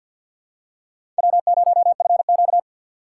Morse code, heard as a keyed tone, about 49 wpm, 700 Hz. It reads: W0FQ
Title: USA Call Signs - 4x Character Spacing - 50wpm